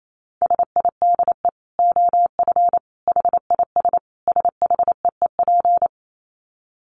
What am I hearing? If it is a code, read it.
Morse code, heard as a keyed tone, about 28 wpm, 700 Hz. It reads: SIDE OF HIS SHEEP
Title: Dit Challenge 28wpm